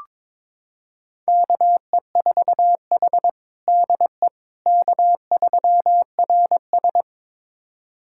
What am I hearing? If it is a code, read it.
Morse code, heard as a keyed tone, about 22 wpm, 700 Hz. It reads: KE4H DE K3RS
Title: USA Callsign DE Callsign - 4 Characters - 22wpm